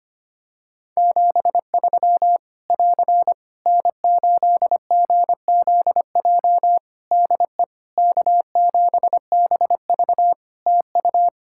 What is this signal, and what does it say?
Morse code, heard as a keyed tone, about 25 wpm, 700 Hz. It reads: 73 <AR> N8GZJ DE K7BV TU
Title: Practice QSO - Exchange 4 - Ending - 25wpm